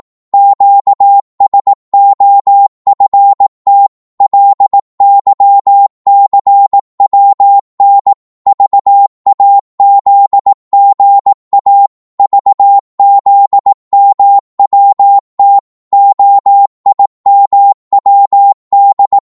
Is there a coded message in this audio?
Yes: Morse code from 0.3 to 19.2 s, heard as a keyed tone, about 18 wpm, 800 Hz. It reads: QSOFT LYCWN VAZGA VZMWT OIMWD